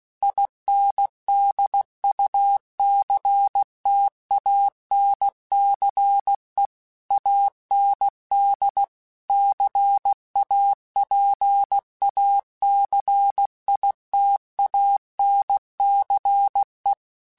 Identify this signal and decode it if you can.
Morse code, heard as a keyed tone, about 16 wpm, 800 Hz. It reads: INDUCTANCE AND CAPACITANCE
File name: Technical Subjects / Technical Subjects-mp3s / Reactance @15